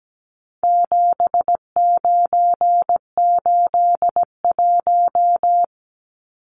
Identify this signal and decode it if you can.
Morse code, heard as a keyed tone, about 17 wpm, 700 Hz. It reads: 7981
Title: Numbers - 4-Digit 17wpm